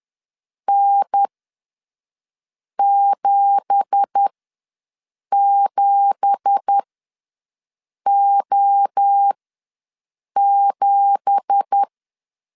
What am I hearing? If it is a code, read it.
Morse code, heard as a keyed tone, about 10 wpm, spaced as for 5 wpm, 800 Hz. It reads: N77O7
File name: LEZIONI / 9 2867 ws 10wpm 4215